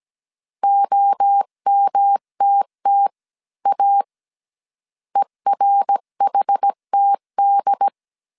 Morse code, heard as a keyed tone, about 17 wpm, 800 Hz. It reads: OMTTA ERHTD